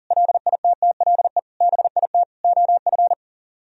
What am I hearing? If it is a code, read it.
Morse code, heard as a keyed tone, about 52 wpm, 700 Hz. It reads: LITTLE BIT OF